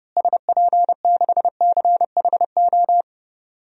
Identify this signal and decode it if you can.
Morse code, heard as a keyed tone, about 30 wpm, 700 Hz. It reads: SP6CHO
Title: DXCC-100 Call Signs - 6 Characters - Repeated 3x - 30wpm